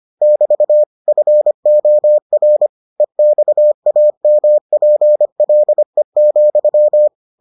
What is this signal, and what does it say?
Morse code, heard as a keyed tone, about 25 wpm, 600 Hz. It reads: <BT> FOR EXAMPLE,